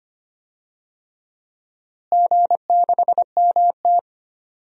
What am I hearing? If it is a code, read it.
Morse code, heard as a keyed tone, about 25 wpm, 700 Hz. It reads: G6MT
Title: DXCC-100 Call Signs - 4 Characters - Repeated 3x - 25wpm